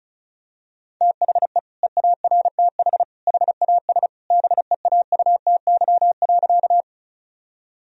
Morse code, heard as a keyed tone, about 34 wpm, 700 Hz. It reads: THE EARTH HAS BEAUTY.
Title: Sentences from Top 1000 Words - Review Part 2 - 35wpm